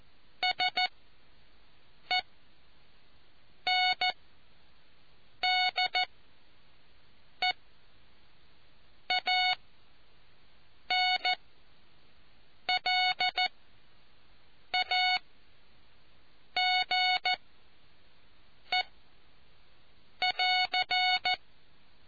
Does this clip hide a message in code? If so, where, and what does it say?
Yes: Morse code from 0.4 to 21.3 s, heard as a keyed tone, about 14 wpm, spaced as for 5 wpm, 740 Hz. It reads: SENDEANLAGE<AR>